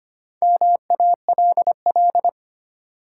Morse code, heard as a keyed tone, about 25 wpm, 700 Hz. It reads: MALL
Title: Four Letter Words - 25wpm